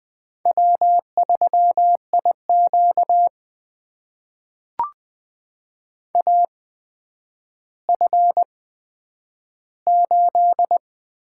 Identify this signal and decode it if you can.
Morse code, heard as a keyed tone, about 20 wpm, spaced as for 10 wpm, 700 Hz. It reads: W3IQ AF8
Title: USA Call Signs - 8x Character Spacing - 20wpm